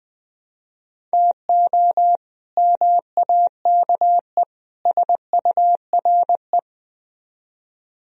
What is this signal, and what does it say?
Morse code, heard as a keyed tone, about 21 wpm, 700 Hz. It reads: TO MAKE SURE
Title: Speed-Racing - Sets of 3 Words 17wpm